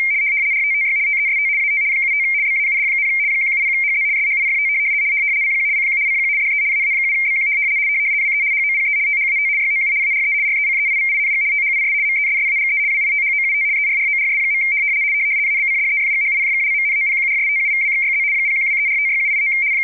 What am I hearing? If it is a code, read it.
RTTY, about 75 baud. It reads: RTTY TEST AT 75 BAUD, 170HZ SHIFT CQ CQ CQ DE AA7BQ WE HOPE THAT YOU ARE RECEIVING THIS MESSAGE WITHOUT ERRORS. ABCDEFGHIJKLMNOPQRSTUVWXYZ 0123456789